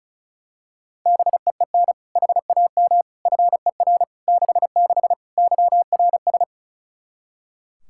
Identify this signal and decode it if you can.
Morse code, heard as a keyed tone, about 34 wpm, 700 Hz. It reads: BEEN HAM FER 66 YRS
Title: Practice QSO - Exchange 3 - More Info - 35wpm